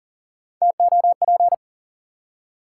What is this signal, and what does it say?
Morse code, heard as a keyed tone, about 40 wpm, 700 Hz. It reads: TOP
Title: Three Letter Words - 40wpm